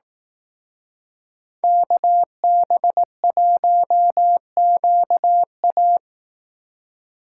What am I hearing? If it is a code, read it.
Morse code, heard as a keyed tone, about 18 wpm, 700 Hz. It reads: KB1QA
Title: Speed Racing - USA 2x2 Call Signs 15wpm